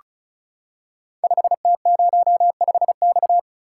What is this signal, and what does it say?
Morse code, heard as a keyed tone, about 36 wpm, 700 Hz. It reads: 5T05X
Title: Code Groups - 5 Letter:Numbers - 35wpm